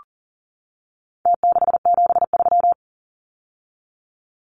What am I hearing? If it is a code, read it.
Morse code, heard as a keyed tone, about 40 wpm, 700 Hz. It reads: T673
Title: Numbers - Contest Style - 4-Digit 40wpm